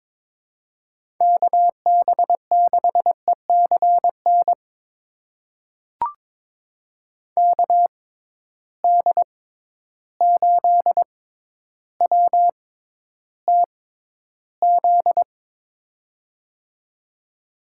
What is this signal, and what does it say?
Morse code, heard as a keyed tone, about 22 wpm, spaced as for 11 wpm, 700 Hz. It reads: KB6ECN KD8WTZ
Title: USA Call Signs - 6 Characters - 6x Character Spacing - 22wpm